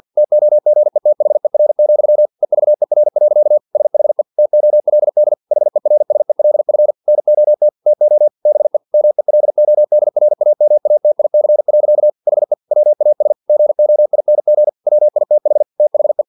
Morse code, heard as a keyed tone, about 50 wpm, 600 Hz. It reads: TOGETHER, EVER. SHE TOLD HERSELF NOT TO BE MELODRAMATIC. HE WAS GOING WITH THE
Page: Never 49wpm